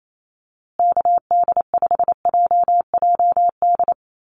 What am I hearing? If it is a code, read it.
Morse code, heard as a keyed tone, about 28 wpm, 700 Hz. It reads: KD5JJD
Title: USA Call Signs Repeated 3x 28wpm